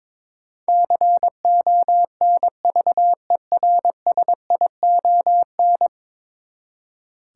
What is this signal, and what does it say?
Morse code, heard as a keyed tone, about 22 wpm, 700 Hz. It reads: CONVERSION
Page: morse-code-ninja-practice-22wpm